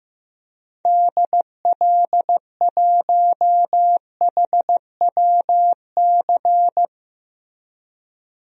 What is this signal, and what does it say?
Morse code, heard as a keyed tone, about 15 wpm, 700 Hz. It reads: DL1HWC
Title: DXCC-100 Call Signs - 6 Characters - Repeated 3x - 15wpm